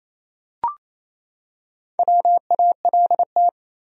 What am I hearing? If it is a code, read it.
Morse code, heard as a keyed tone, about 28 wpm, 700 Hz. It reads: WALT
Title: USA Common Names - 28wpm